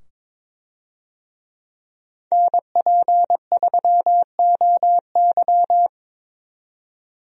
Morse code, heard as a keyed tone, about 22 wpm, 700 Hz. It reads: NP3OY
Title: USA Call Signs - 5 Characters - 22wpm